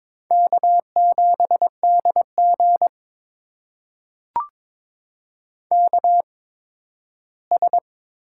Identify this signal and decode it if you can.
Morse code, heard as a keyed tone, about 22 wpm, spaced as for 13 wpm, 700 Hz. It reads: K7DG KS
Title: USA Call Signs - 4 Characters - 8x Character Spacing - 22wpm